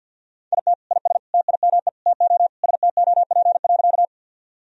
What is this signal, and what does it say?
Morse code, heard as a keyed tone, about 50 wpm, 700 Hz. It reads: IT IS TIME TO STOP.
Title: Sentences from Top 1000 Words - Review Part 1 - 50wpm